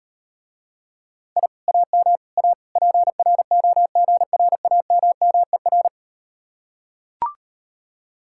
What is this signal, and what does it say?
Morse code, heard as a keyed tone, about 39 wpm, 700 Hz. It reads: I AM A PROGRAMMER